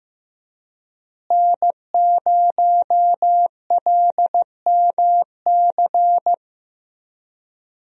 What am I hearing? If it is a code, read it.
Morse code, heard as a keyed tone, about 15 wpm, 700 Hz. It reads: N0LMC
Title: USA Call Signs - 5 Characters - Repeated 3x - 15wpm